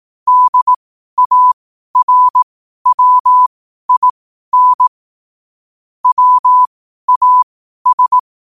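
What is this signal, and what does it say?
Morse code, heard as a keyed tone, about 18 wpm, spaced as for 12 wpm, 1000 Hz. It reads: DARWIN WAS
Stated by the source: Morse code at 12 wpm.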